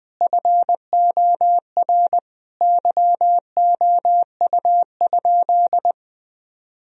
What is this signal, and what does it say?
Morse code, heard as a keyed tone, about 20 wpm, 700 Hz. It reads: FOR YOU?
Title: Sentences from Top 1000 Words - Review Part 4 - 20wpm